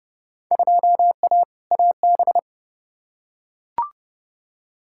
Morse code, heard as a keyed tone, about 30 wpm, 700 Hz. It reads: 2A AB
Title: Field Day Exchanges - 30wpm